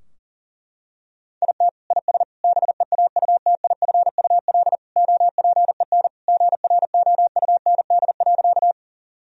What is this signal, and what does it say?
Morse code, heard as a keyed tone, about 40 wpm, 700 Hz. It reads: IT IS BEAUTIFUL OPEN GROUND.
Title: Sentences from Top 1000 Words - Review Part 2 - 40wpm